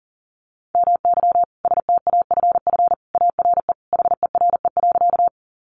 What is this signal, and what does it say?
Morse code, heard as a keyed tone, about 40 wpm, 700 Hz. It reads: MY STAFF ARE HERE.
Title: Sentences from Top 1200 Words 40wpm